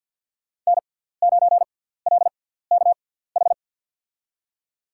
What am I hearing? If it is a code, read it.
Morse code, heard as a keyed tone, about 52 wpm, spaced as for 17 wpm, 700 Hz. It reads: N9LKH
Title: USA Call Signs - 5 Characters - 6x Character Spacing - 50wpm